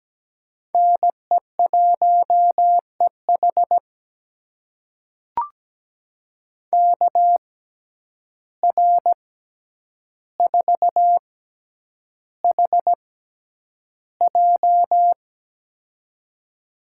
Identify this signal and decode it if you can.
Morse code, heard as a keyed tone, about 17 wpm, spaced as for 9 wpm, 700 Hz. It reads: NE1EH KR4HJ
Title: USA Call Signs - 5 Characters - 6x Character Spacing - 17wpm